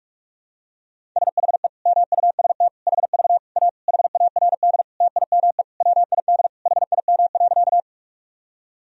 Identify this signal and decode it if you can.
Morse code, heard as a keyed tone, about 41 wpm, 700 Hz. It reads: SHE MUST HV A HARD TIME WID HIM.
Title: Sentences from Top 300 Words and Abbreviations 45wpm